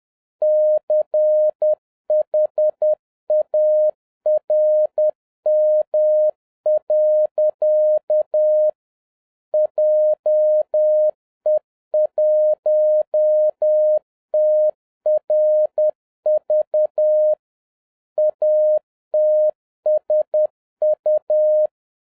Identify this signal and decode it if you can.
Morse code, heard as a keyed tone, about 10 wpm, 610 Hz. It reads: CHARM. JE1TRV ATSU